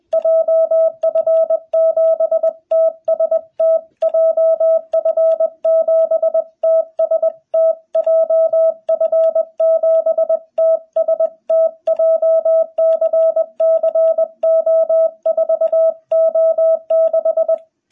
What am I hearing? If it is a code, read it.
Morse code, heard as a keyed tone, about 20 wpm, 660 Hz. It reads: JF7TST JF7TST JF7TST JCC O4O6
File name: jh8cbh / 2021cw